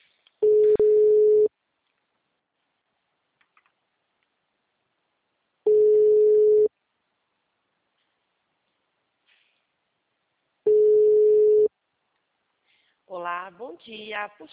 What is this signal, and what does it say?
Ringback tone (Europe, 425 Hz), with speech.
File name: out- -2001- - - .260